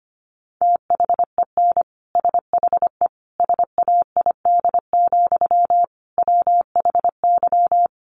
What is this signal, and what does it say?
Morse code, heard as a keyed tone, about 25 wpm, 700 Hz. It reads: THEN SHE SAID, WHY
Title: Sentences from Top 800 Words 25wpm